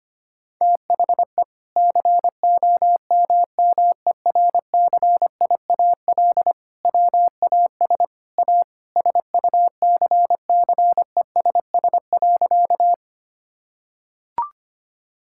Morse code, heard as a keyed tone, about 25 wpm, 700 Hz. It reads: THE COMMERCIAL WAS A SUCCESS.
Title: Sentences from Top 1000 Words - Review Part 5 - 25wpm